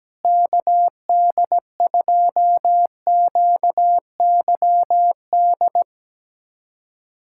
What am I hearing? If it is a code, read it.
Morse code, heard as a keyed tone, about 17 wpm, 700 Hz. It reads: KD2QYD
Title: USA Callsign DE Callsign - 6 Characters - 17wpm